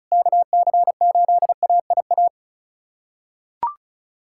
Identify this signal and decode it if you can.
Morse code, heard as a keyed tone, about 36 wpm, 700 Hz. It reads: KC8AIA
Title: SOTA QSO Practice 35wpm